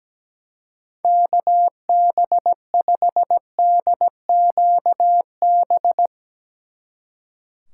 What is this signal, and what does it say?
Morse code, heard as a keyed tone, about 17 wpm, 700 Hz. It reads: KB5DQB
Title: Speed Racing - USA Call Signs 17wpm